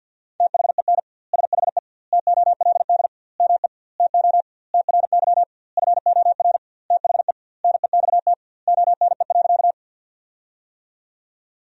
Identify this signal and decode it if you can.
Morse code, heard as a keyed tone, about 50 wpm, 700 Hz. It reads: THEN SHE TOLD ME TO TRY FOR THE NEXT ONE.
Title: 2x Word-Spacing - Sentences from Top 400 Words - 50wpm